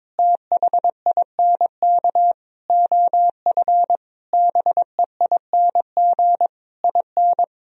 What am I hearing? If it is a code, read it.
Morse code, heard as a keyed tone, about 22 wpm, 700 Hz. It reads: THINK OF BEING IN
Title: Sentences from Top 1000 Words - Review Part 3 - 22wpm